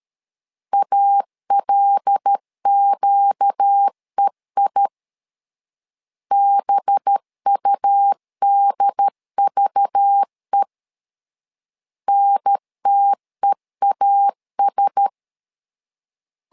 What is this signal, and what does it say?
Morse code, heard as a keyed tone, about 13 wpm, 800 Hz. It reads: ALQEI BUDVE NTEAS
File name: LEZIONI / 5 LBVC / 5 LBV 12wpm 1855